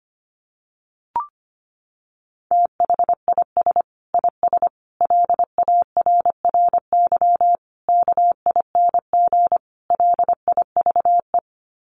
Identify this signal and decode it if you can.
Morse code, heard as a keyed tone, about 25 wpm, 700 Hz. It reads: THIS IS LARRY KING LIVE
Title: Sets of 5 Words Encore2 25wpm